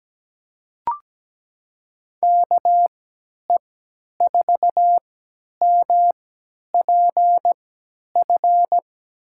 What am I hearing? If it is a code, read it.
Morse code, heard as a keyed tone, about 17 wpm, spaced as for 10 wpm, 700 Hz. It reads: KE4MPF
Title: USA Call Signs - 6 Characters - 3x Character Spacing - 17wpm